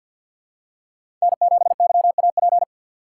Morse code, heard as a keyed tone, about 47 wpm, 700 Hz. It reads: N7YAP
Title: USA Call Signs - 5 Characters - 4x Character Spacing - 50wpm